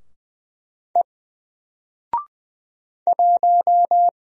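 Morse code, heard as a keyed tone, about 20 wpm, 700 Hz. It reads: E 1
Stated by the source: Single letter and number Morse drill at 20 wpm, from N4LCD.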